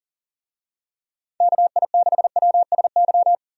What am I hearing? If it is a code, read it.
Morse code, heard as a keyed tone, about 40 wpm, 700 Hz. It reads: KI6WSY
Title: USA Call Signs - 6 Characters - 40wpm